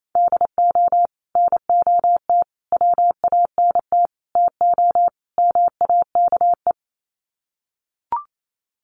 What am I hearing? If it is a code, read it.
Morse code, heard as a keyed tone, about 28 wpm, 700 Hz. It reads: DO NOT WANT TO MAKE
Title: Sets of 5 Words Encore 28wpm